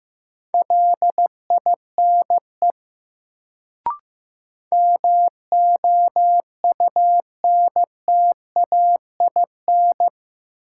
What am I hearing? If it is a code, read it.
Morse code, heard as a keyed tone, about 15 wpm, 700 Hz. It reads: LINE MOUNTAIN
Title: Sentences from Top 1000 Words - Review Part 1 - New Word Drill - 15wpm